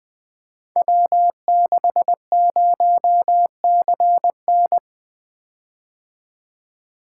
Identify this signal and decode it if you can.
Morse code, heard as a keyed tone, about 20 wpm, 700 Hz. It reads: W60CN
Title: Code Groups - 5 Letter:Numbers - 20wpm